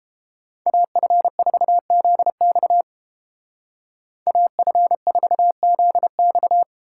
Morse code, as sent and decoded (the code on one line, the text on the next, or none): .- ..-. ....- --.. -..- / .- ..-. ....- --.. -..-
AF4ZX AF4ZX